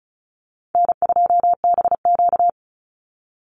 Morse code, heard as a keyed tone, about 34 wpm, 700 Hz. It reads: N2BQ
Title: USA Call Signs 35wpm